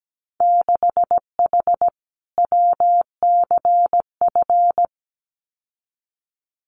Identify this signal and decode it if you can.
Morse code, heard as a keyed tone, about 17 wpm, 700 Hz. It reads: 6H WCF
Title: Winter Field Day Exchanges Encore 17wpm